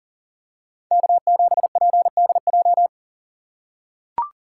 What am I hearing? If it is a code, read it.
Morse code, heard as a keyed tone, about 40 wpm, 700 Hz. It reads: K7PDJ
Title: USA Call Signs - 5 Characters - 3x Character Spacing - 40wpm